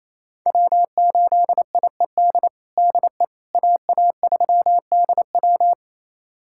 Morse code, heard as a keyed tone, about 28 wpm, 700 Hz. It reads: W8IED DE AA3DW
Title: USA Callsign DE Callsign - 5 Characters - 28wpm